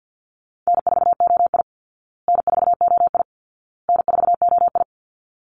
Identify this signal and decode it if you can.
Morse code, heard as a keyed tone, about 50 wpm, 700 Hz. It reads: N4OI N4OI N4OI
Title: Easy Call Signs Repeated 3x - 50wpm